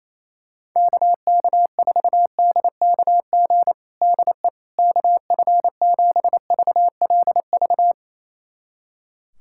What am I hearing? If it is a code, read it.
Morse code, heard as a keyed tone, about 28 wpm, 700 Hz. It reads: KK4DKG DE KF7VLV
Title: USA Callsign DE Callsign 28wpm